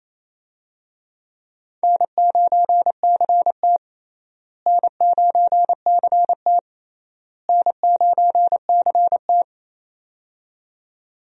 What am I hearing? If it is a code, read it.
Morse code, heard as a keyed tone, about 28 wpm, 700 Hz. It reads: N9CT N9CT N9CT
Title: USA Call Signs - 4 Characters - Repeated 3x - 28wpm